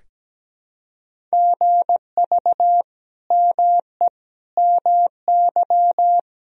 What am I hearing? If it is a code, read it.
Morse code, heard as a keyed tone, about 17 wpm, 700 Hz. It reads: GV ME MY
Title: Sentences from Top 200 Words and Abbreviations 17wpm